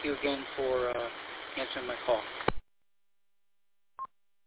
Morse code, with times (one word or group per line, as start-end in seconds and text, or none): none